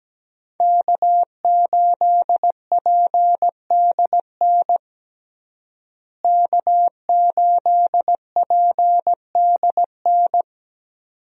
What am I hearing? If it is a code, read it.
Morse code, heard as a keyed tone, about 17 wpm, 700 Hz. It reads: K8PDN K8PDN
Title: USA Call Signs - 5 Characters - Repeated 2x - 17wpm